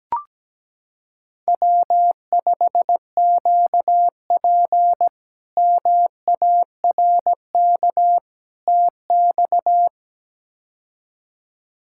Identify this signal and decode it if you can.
Morse code, heard as a keyed tone, about 17 wpm, 700 Hz. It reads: W5QP MARK TX
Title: K1USN SST QSO Practice - 17wpm